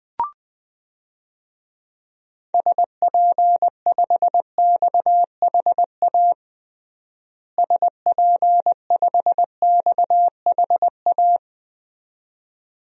Morse code, heard as a keyed tone, about 21 wpm, 700 Hz. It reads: SP5XHA SP5XHA
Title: DXCC-100 Call Signs - 6 Characters - Repeated 2x - 20wpm